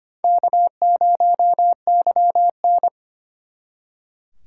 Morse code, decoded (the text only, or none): K0YN